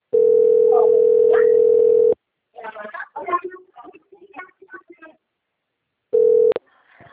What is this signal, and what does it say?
Ringback tone (North America), 440 + 480 Hz, with speech.
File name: out- - - - - .22262